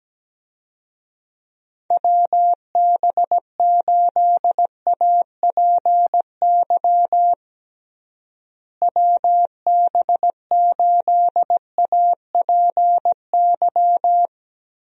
Morse code, heard as a keyed tone, about 17 wpm, 700 Hz. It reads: WB8APY WB8APY
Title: USA Call Signs - 6 Characters - Repeated 2x - 17wpm